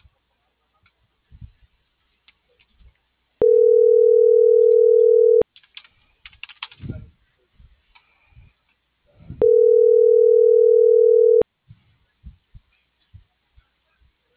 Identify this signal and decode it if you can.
Ringback tone (North America), 440 + 480 Hz.